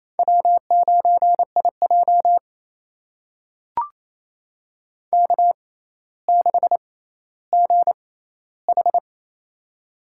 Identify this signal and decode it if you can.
Morse code, heard as a keyed tone, about 28 wpm, spaced as for 14 wpm, 700 Hz. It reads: W9IJ K6GH
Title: USA Call Signs - 6x Character Spacing - 28wpm